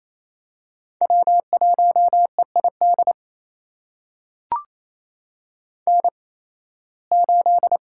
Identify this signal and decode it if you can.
Morse code, heard as a keyed tone, about 27 wpm, spaced as for 18 wpm, 700 Hz. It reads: W1EID N8
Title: USA Call Signs - 8x Character Spacing - 28wpm